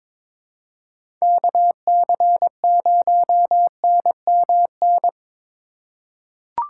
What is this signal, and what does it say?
Morse code, heard as a keyed tone, about 23 wpm, 700 Hz. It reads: KC0NMN
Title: USA Call Signs - 6 Characters - 3x Character Spacing - 22wpm